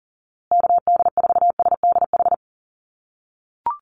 Morse code, heard as a keyed tone, about 40 wpm, 700 Hz. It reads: KD4SDH
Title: USA Call Signs Repeated 2x 40wpm